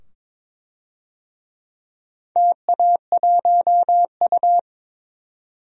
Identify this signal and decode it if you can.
Morse code, heard as a keyed tone, about 22 wpm, 700 Hz. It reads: TA1U
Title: DXCC-100 Call Signs - 4 Characters - 22wpm